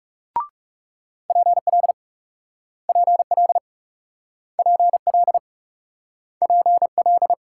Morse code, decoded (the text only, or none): PL PL PL PL